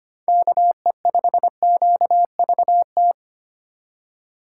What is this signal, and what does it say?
Morse code, heard as a keyed tone, about 25 wpm, 700 Hz. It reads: KE5QVT
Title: USA Call Signs - 6 Characters - 2x Character Spacing - 25wpm